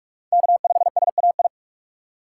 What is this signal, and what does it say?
Morse code, heard as a keyed tone, about 44 wpm, 700 Hz. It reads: K5SAI